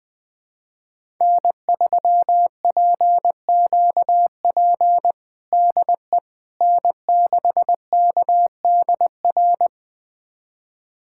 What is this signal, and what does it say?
Morse code, heard as a keyed tone, about 20 wpm, 700 Hz. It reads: N3PQP DE N6KDR
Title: USA Callsign DE Callsign - 5 Characters - 20wpm